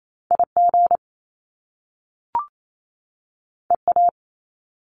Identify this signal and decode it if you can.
Morse code, heard as a keyed tone, about 28 wpm, 700 Hz. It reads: IG EA
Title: Top Two Letter Combinations in Words 28wpm